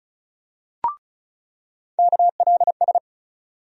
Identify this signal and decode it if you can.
Morse code, heard as a keyed tone, about 34 wpm, 700 Hz. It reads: KLS
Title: USA Call Sign Postfixes - 3 Letters 35wpm